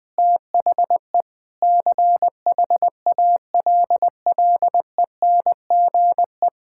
Morse code, heard as a keyed tone, about 21 wpm, 700 Hz. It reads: THE CHALLENGE
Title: Sentences with 5-Word Phrases - Spoken Beginning - 20wpm